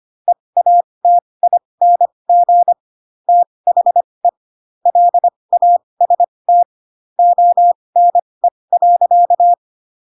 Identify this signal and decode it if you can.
Morse code, heard as a keyed tone, about 25 wpm, spaced as for 20 wpm, 700 Hz. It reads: EATING THE LAST ONE.
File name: Pinocchio / Pinocchio ch7 15 25 ik1hge